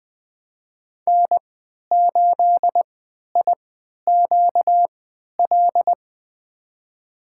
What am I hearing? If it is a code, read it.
Morse code, heard as a keyed tone, about 20 wpm, spaced as for 11 wpm, 700 Hz. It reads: N8IQL
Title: USA Call Signs - 5 Characters - 3x Character Spacing - 20wpm